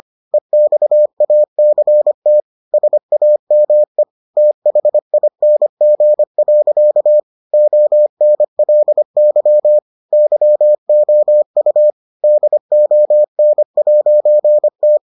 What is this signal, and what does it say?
Morse code, heard as a keyed tone, about 25 wpm, 600 Hz. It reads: EXACT SAME THING. ONLY YOU DON'T